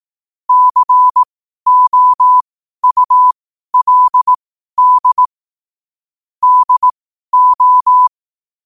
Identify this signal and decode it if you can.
Morse code, heard as a keyed tone, about 18 wpm, spaced as for 12 wpm, 1000 Hz. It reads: COULD DO
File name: qotdfiles / 12wpm